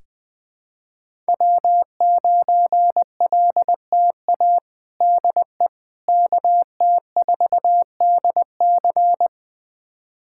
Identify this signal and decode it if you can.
Morse code, heard as a keyed tone, about 20 wpm, 700 Hz. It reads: W9LTA DE KT4DC
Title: USA Callsign DE Callsign - 5 Characters - 20wpm